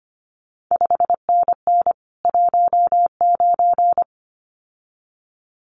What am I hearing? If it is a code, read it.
Morse code, heard as a keyed tone, about 25 wpm, 700 Hz. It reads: 5NN 19
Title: CQ WW DX Contest Exchange Encore - 25wpm